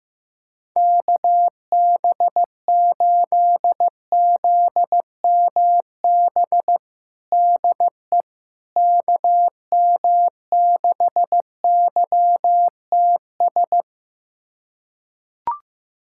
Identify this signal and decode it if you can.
Morse code, heard as a keyed tone, about 16 wpm, 700 Hz. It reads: KB8ZMB DE KM6YTS
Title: USA Callsign DE Callsign - 6 Characters - 15wpm